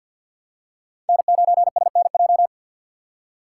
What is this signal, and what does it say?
Morse code, heard as a keyed tone, about 53 wpm, 700 Hz. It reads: N9SNJ
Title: USA Call Signs - 5 Characters - 6x Character Spacing - 50wpm